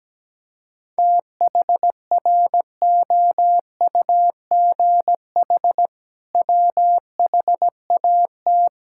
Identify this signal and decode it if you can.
Morse code, heard as a keyed tone, about 17 wpm, 700 Hz. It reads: THROUGH WHAT
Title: Sentences from Top 1000 Words - Review Part 6 - 17wpm